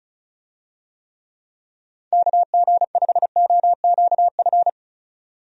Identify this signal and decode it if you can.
Morse code, heard as a keyed tone, about 36 wpm, 700 Hz. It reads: KG5OQF
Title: USA Call Signs - 6 Characters - 35wpm